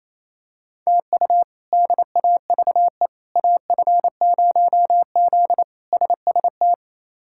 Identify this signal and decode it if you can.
Morse code, heard as a keyed tone, about 28 wpm, 700 Hz. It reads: TU DAVE AF0Z SST
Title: K1USN SST QSO Practice - 28wpm